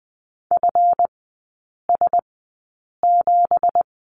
Morse code, heard as a keyed tone, about 20 wpm, spaced as for 8 wpm, 700 Hz. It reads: FS7
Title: 166 Lesson 27e - Number 7 - Three Characters 20wpm